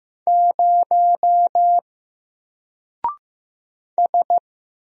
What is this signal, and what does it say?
Morse code, heard as a keyed tone, about 15 wpm, 700 Hz. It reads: 0 S
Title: Single Character (Letter:Number:Punctuation:Prosign) 15wpm